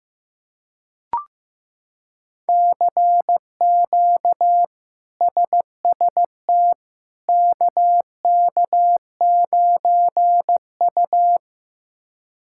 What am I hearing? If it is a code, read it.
Morse code, heard as a keyed tone, about 16 wpm, 700 Hz. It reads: CQ SST KK9U
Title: K1USN SST QSO Practice - 15wpm